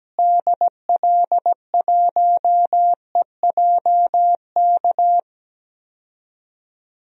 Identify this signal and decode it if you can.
Morse code, heard as a keyed tone, about 17 wpm, 700 Hz. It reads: DL1EJK
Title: DXCC-100 Call Signs - 6 Characters - Repeated 3x - 17wpm